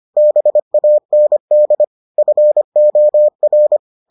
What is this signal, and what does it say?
Morse code, heard as a keyed tone, about 25 wpm, 600 Hz. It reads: BAND FOR